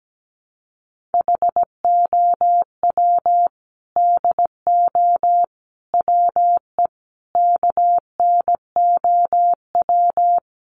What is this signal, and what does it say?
Morse code, heard as a keyed tone, about 17 wpm, 700 Hz. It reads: HOW DO WE KNOW
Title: A Sound Of Thunder 17wpm